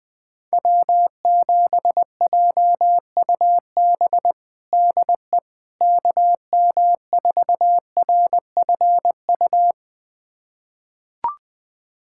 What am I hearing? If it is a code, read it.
Morse code, heard as a keyed tone, about 20 wpm, 700 Hz. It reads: W7JUB DE KM4RFU
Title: USA Callsign DE Callsign 20wpm